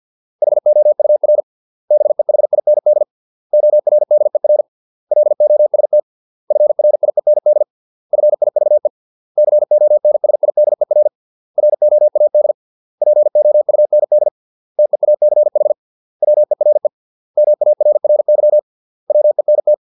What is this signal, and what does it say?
Morse code, heard as a keyed tone, about 50 wpm, 600 Hz. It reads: HOUR BEHIND ORDER LOST FRIEND FIVE CONSIDER ROAD POUND TEACH WERE MARRY WENT